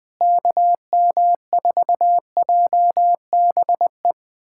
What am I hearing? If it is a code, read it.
Morse code, heard as a keyed tone, about 21 wpm, 700 Hz. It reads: KM4JBE
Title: USA Callsign DE Callsign - 6 Characters - 20wpm